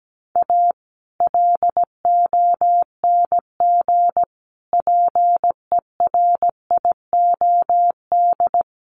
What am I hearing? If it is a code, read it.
Morse code, heard as a keyed tone, about 17 wpm, 700 Hz. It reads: A LONG PERIOD
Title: Sets of 5 Words 17wpm